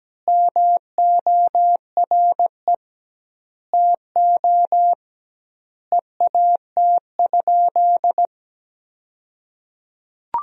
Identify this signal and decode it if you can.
Morse code, heard as a keyed tone, about 17 wpm, 700 Hz. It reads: MORE TO EAT?
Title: 2x Word-Spacing - Sentences from Top 300 Words - 17wpm